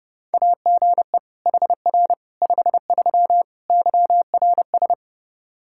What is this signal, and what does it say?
Morse code, heard as a keyed tone, about 32 wpm, 700 Hz. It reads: AGE HR 53 YRS
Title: Practice QSO - Exchange 3 - More Info - 30wpm